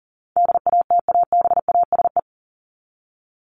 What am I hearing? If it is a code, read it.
Morse code, heard as a keyed tone, about 40 wpm, 700 Hz. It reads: DATABASE
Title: Top 1751-2000 Words 40wpm